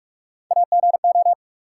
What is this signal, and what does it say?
Morse code, heard as a keyed tone, about 44 wpm, 700 Hz. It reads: AGO
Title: Sets of 2 Words - New Word Drill - 45wpm